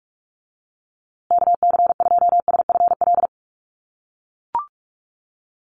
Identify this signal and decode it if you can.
Morse code, heard as a keyed tone, about 44 wpm, 700 Hz. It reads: KC2SFL
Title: USA Call Signs Repeated 2x 45wpm